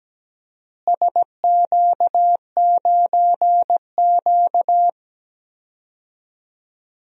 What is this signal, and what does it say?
Morse code, heard as a keyed tone, about 17 wpm, 700 Hz. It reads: SQ9Q
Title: DXCC-100 Call Signs - 4 Characters - Repeated 3x - 17wpm